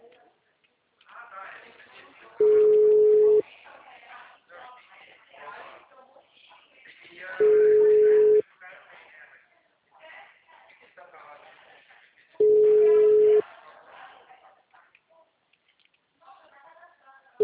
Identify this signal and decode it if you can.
Ringback tone (Europe, 425 Hz), with speech.